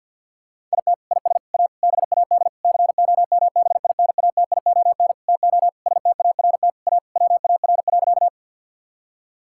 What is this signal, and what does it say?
Morse code, heard as a keyed tone, about 50 wpm, 700 Hz. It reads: IT IS A BAD COMBINATION TO START A WAR.
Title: Sentences from Top 1000 Words - Review Part 6 - 50wpm